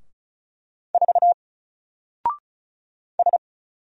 Morse code, heard as a keyed tone, about 34 wpm, 700 Hz. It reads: 4 S
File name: Single-Letter-Number-35wpm